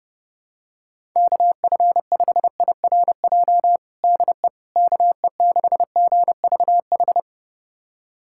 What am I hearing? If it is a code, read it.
Morse code, heard as a keyed tone, about 30 wpm, 700 Hz. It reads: KF5IRJ DE KE6GVH
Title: USA Callsign DE Callsign - 6 Characters - 30wpm